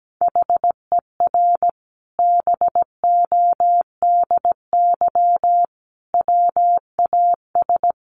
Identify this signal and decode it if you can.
Morse code, heard as a keyed tone, about 17 wpm, 700 Hz. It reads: HER BODY WAS
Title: Sentences from Top 1400 Words 17wpm